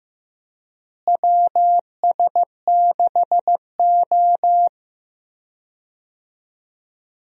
Morse code, heard as a keyed tone, about 16 wpm, 700 Hz. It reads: WS6O
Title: USA Call Signs - 4 Characters - Repeated 3x - 15wpm